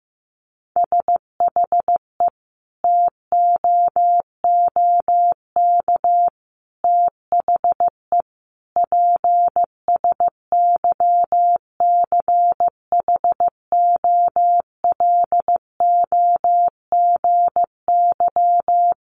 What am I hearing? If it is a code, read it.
Morse code, heard as a keyed tone, about 15 wpm, 700 Hz. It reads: SHE TOOK THE PSYCHOLOGY
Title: Sentences from Top 1400 Words 15wpm